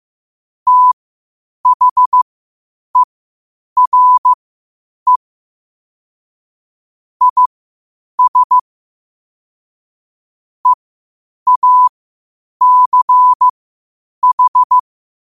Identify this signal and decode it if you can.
Morse code, heard as a keyed tone, about 15 wpm, spaced as for 8 wpm, 1000 Hz. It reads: THERE IS EACH
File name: qotdfiles / 8wpm